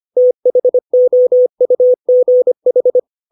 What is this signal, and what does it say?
Morse code, heard as a keyed tone, about 25 wpm, 500 Hz. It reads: THOUGH
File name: 632 words / 632 Words 25WPM